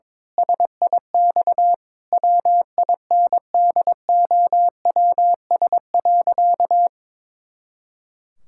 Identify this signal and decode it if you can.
Morse code, heard as a keyed tone, about 22 wpm, 700 Hz. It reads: SIX WINDOWS.
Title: Pangram Challenge v2 - 22wpm